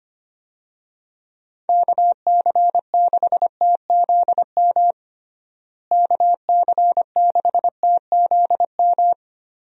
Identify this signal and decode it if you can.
Morse code, heard as a keyed tone, about 25 wpm, 700 Hz. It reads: KC6TZM KC6TZM
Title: USA Call Signs - 6 Characters - Repeated 2x - 25wpm